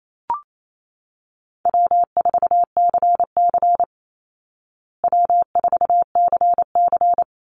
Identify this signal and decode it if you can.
Morse code, heard as a keyed tone, about 28 wpm, 700 Hz. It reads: W4CC W4CC
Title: Easy Call Signs Repeated 2x - 28wpm